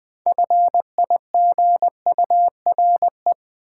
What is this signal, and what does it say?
Morse code, heard as a keyed tone, about 20 wpm, 700 Hz. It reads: FIGURE
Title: Sentences from Top 1400 Words - New Word Drill Alt - 20wpm